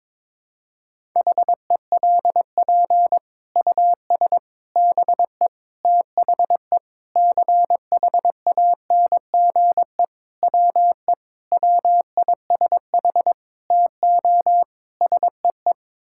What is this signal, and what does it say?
Morse code, heard as a keyed tone, about 22 wpm, 700 Hz. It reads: HELP US BE THE CHANGE WE WISH TO SEE
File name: Sets of 9 Words - 22wpm